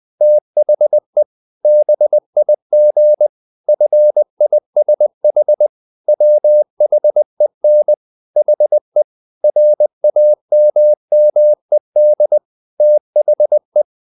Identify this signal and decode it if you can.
Morse code, heard as a keyed tone, about 20 wpm, 600 Hz. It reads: THE BIG FISH WHEN HE RAMMED THE
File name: the old man and the sea part 2 20wpm